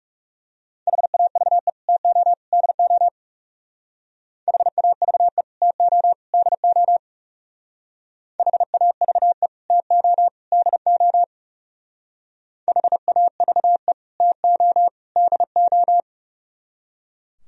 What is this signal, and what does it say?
Morse code, heard as a keyed tone, about 36 wpm, 700 Hz. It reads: HAVE TO DO HAVE TO DO HAVE TO DO HAVE TO DO
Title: Speed-Racing - Sets of 3 Words 30wpm